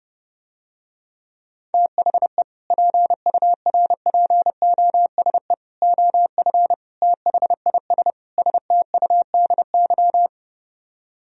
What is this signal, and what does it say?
Morse code, heard as a keyed tone, about 30 wpm, 700 Hz. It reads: THE PURPOSE OF THIS STUDY